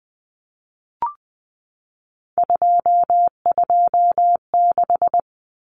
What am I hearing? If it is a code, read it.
Morse code, heard as a keyed tone, about 20 wpm, 700 Hz. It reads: 226
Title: Numbers - 3-Digit 20wpm